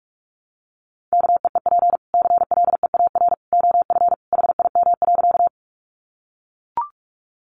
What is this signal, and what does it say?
Morse code, heard as a keyed tone, about 44 wpm, 700 Hz. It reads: KEEP CLEAR OF HIM.
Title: Sentences from Top 500 Words 45wpm